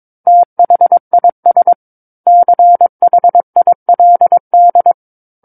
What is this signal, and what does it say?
Morse code, heard as a keyed tone, about 22 wpm, 700 Hz. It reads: THIS CHILD